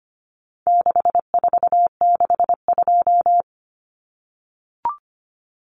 Morse code, heard as a keyed tone, about 25 wpm, 700 Hz. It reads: 6462
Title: Numbers - 4-Digit 25wpm